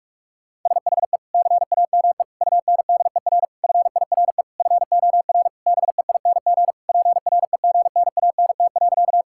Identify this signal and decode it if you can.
Morse code, heard as a keyed tone, about 46 wpm, 700 Hz. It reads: SHE CAME UNDER FIRE FOR BEING PREGNANT.
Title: Sentences from Top 1000 Words - Review Part 6 - 45wpm